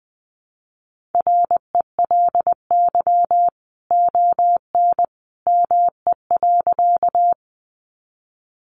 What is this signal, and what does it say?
Morse code, heard as a keyed tone, about 20 wpm, 700 Hz. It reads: RELY ON ME.
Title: Sentences from Top 1200 Words 20wpm